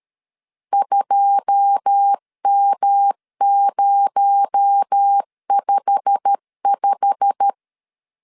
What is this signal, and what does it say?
Morse code, heard as a keyed tone, about 13 wpm, 800 Hz. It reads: 2M055
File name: CW / DET / 1 9 1500C S C 12wpm 3811